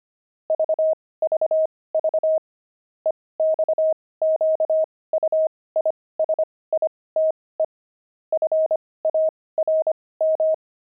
Morse code, heard as a keyed tone, about 25 wpm, spaced as for 18 wpm, 630 Hz. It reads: VVV EXQUISITE FARM
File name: pangram-fs-18-cs-25-jolt